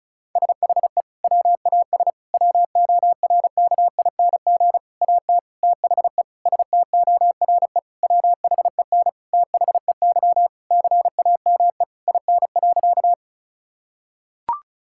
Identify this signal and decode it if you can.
Morse code, heard as a keyed tone, about 36 wpm, 700 Hz. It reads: SHE WAS WORKING AT THE STORE WHEN THEY CAME IN.
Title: Sentences from Top 1000 Words - Review Part 5 - 35wpm